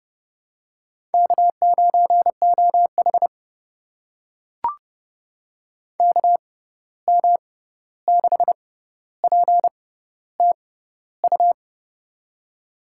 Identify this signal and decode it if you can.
Morse code, heard as a keyed tone, about 30 wpm, spaced as for 14 wpm, 700 Hz. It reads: K9OH KM6PTU
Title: USA Call Signs - 6x Character Spacing - 30wpm